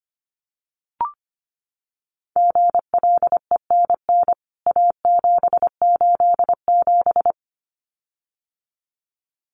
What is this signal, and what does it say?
Morse code, heard as a keyed tone, about 25 wpm, 700 Hz. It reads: GLENN A787
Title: CWT Exchange 25wpm